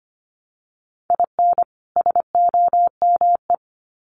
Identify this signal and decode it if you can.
Morse code, heard as a keyed tone, about 25 wpm, 700 Hz. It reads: IN SOME
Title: Sets of 2 Words 25wpm